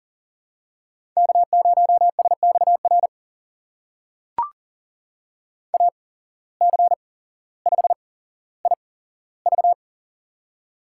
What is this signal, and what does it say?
Morse code, heard as a keyed tone, about 37 wpm, spaced as for 17 wpm, 700 Hz. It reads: K0SXR AC5IV
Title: USA Call Signs - 5 Characters - 8x Character Spacing - 40wpm